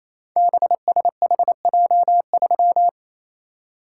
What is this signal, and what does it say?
Morse code, heard as a keyed tone, about 27 wpm, 700 Hz. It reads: BSHJ3
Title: Code Groups - 5 Letter:Numbers - 28wpm